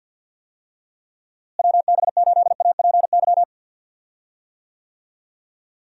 Morse code, heard as a keyed tone, about 46 wpm, 700 Hz. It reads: WB8APY
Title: USA Call Signs - 6 Characters - Repeated 2x - 50wpm